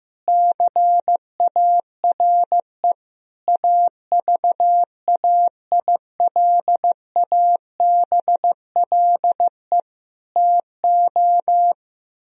Morse code, heard as a keyed tone, about 15 wpm, 700 Hz. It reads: CARE AVAILABLE TO
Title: Sentences from Top 1000 Words - Review Part 3 - 15wpm